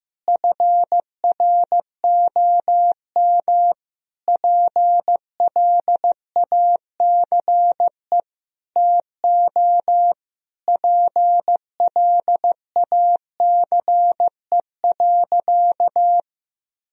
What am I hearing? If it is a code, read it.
Morse code, heard as a keyed tone, about 15 wpm, 700 Hz. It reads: FROM PLACE TO PLACE.
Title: Sentences from Top 1000 Words - Review Part 1 - 15wpm